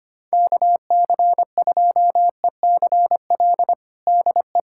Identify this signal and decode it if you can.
Morse code, heard as a keyed tone, about 25 wpm, 700 Hz. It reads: KC2ECL DE
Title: USA Callsign DE Callsign - 6 Characters - 25wpm